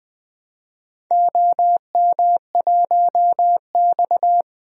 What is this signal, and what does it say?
Morse code, heard as a keyed tone, about 20 wpm, 700 Hz. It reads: OM1X
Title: DXCC-100 Call Signs - 4 Characters - 20wpm